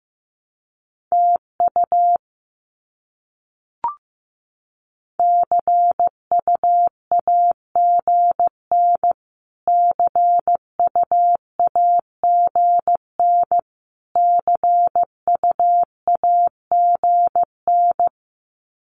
Morse code, heard as a keyed tone, about 15 wpm, 700 Hz. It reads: TU CUAGN CUAGN CUAGN
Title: Abbreviations Repeated 3x - 15wpm